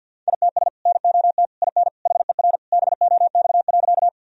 Morse code, heard as a keyed tone, about 52 wpm, 700 Hz. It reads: ITS NOT IN HER BOX.
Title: Sentences from Top 1000 Words - Review Part 2 - 50wpm